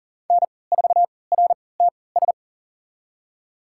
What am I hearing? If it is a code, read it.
Morse code, heard as a keyed tone, about 40 wpm, spaced as for 23 wpm, 700 Hz. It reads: N4RTS
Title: USA Call Signs - 3x Character Spacing - 40wpm